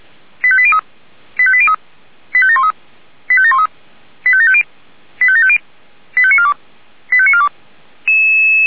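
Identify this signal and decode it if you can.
Five-tone selective call (ZVEI), heard as quick runs of five tones, tones of about 70 ms. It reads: 85803 85803 85713 85713 85750 85750 85942 85942
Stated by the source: ZVEI five-tone selective call 85730 received on SDR.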